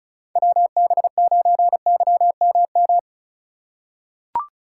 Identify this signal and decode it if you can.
Morse code, heard as a keyed tone, about 36 wpm, 700 Hz. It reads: WB9YMM
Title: USA Call Signs - 6x Character Spacing - 35wpm